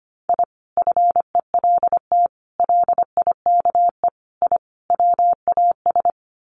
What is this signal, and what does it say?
Morse code, heard as a keyed tone, about 25 wpm, 700 Hz. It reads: I FELT LIKE I WAS
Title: Sets of 5 Words Encore 25wpm